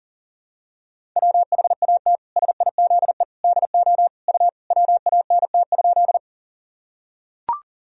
Morse code, heard as a keyed tone, about 40 wpm, 700 Hz. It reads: WHAT SIZE DO U WANT?
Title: Sentences from Top 400 Words and Abbreviations 40wpm